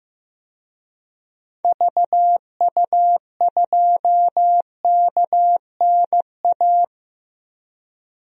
Morse code, heard as a keyed tone, about 15 wpm, 700 Hz. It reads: VU2KNA
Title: DXCC-100 Call Signs - 6 Characters - Repeated 2x - 15wpm